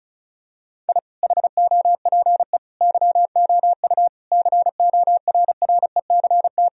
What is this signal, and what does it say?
Morse code, heard as a keyed tone, about 36 wpm, 700 Hz. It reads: I HOPE YOU CORRECT
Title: Sentences from Top 1000 Words - Review Part 2 - 35wpm